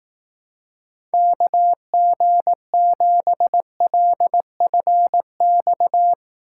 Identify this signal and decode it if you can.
Morse code, heard as a keyed tone, about 18 wpm, 700 Hz. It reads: KG7LFX
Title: Speed Racing - USA 2x3 Call Signs 15wpm